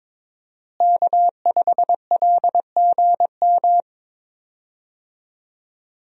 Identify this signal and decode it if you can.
Morse code, heard as a keyed tone, about 21 wpm, 700 Hz. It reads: K5LGM
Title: USA Call Signs - 5 Characters - Repeated 2x - 22wpm